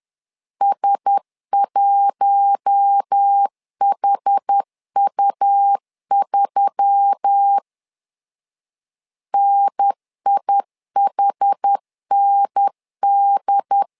Morse code, heard as a keyed tone, about 10 wpm, 800 Hz. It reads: S1HU3 NIHND
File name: LEZIONI / 5 LBVC / 5 L 10wpm 2259